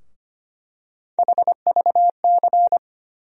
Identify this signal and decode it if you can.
Morse code, heard as a keyed tone, about 25 wpm, 700 Hz. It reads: HVC
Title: USA Call Sign Postfixes - 3 Letters 25wpm